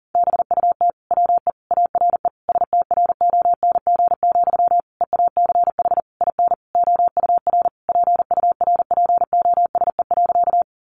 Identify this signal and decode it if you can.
Morse code, heard as a keyed tone, about 40 wpm, 700 Hz. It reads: BUT WE ARE STRONG, EACH IN OUR PURPOSE.
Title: Sentences from Top 800 Words 40wpm